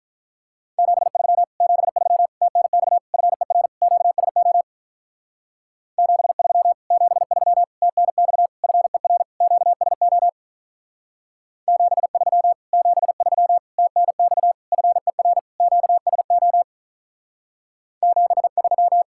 Morse code, heard as a keyed tone, about 45 wpm, 700 Hz. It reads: 73 73 TNX FER QSO 73 73 TNX FER QSO 73 73 TNX FER QSO 73